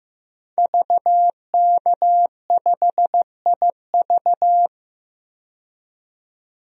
Morse code, heard as a keyed tone, about 15 wpm, 700 Hz. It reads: VK5IV
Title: DXCC-100 Call Signs - 5 Characters - 15wpm